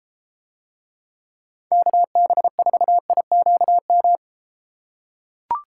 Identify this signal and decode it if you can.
Morse code, heard as a keyed tone, about 32 wpm, 700 Hz. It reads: KB4IQM